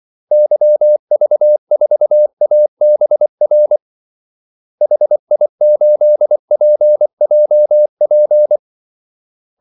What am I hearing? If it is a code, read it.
Morse code, heard as a keyed tone, about 24 wpm, 600 Hz. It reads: YV4ABR HI8PJP
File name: Callsigns-24wpm